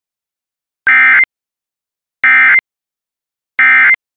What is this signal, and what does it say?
Emergency Alert System (SAME) tones, heard as harsh buzzing bursts. It is heard from 0.8 to 3.9 s.